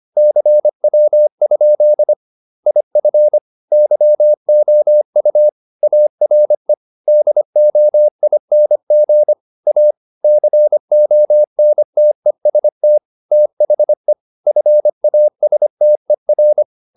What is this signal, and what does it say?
Morse code, heard as a keyed tone, about 25 wpm, 600 Hz. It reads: CW? IF YOU ARE DOING A CONTEST THE FASTER